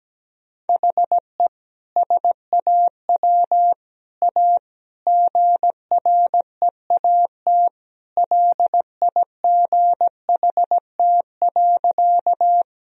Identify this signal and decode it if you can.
Morse code, heard as a keyed tone, about 17 wpm, 700 Hz. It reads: HE SAW A GREAT LIGHT.
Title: Sentences from Top 1000 Words - Review Part 1 - 17wpm